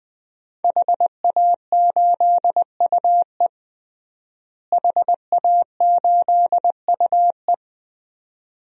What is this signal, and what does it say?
Morse code, heard as a keyed tone, about 20 wpm, 700 Hz. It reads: HA8UE HA8UE
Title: DXCC-100 Call Signs - 5 Characters - Repeated 2x - 20wpm